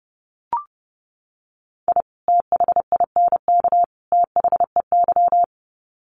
Morse code, heard as a keyed tone, about 30 wpm, 700 Hz. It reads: I THINK THEY
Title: Sets of 3 Words 30wpm